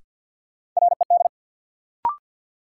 Morse code, heard as a keyed tone, about 54 wpm, 700 Hz. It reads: RED
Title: Sentences from Top 700 Words - New Word Drill - 50wpm